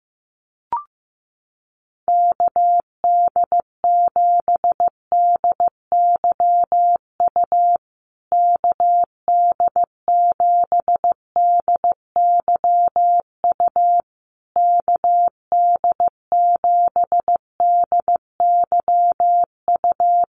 Morse code, heard as a keyed tone, about 14 wpm, 700 Hz. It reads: KD7DYU KD7DYU KD7DYU
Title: USA Call Signs Repeated 3x 15wpm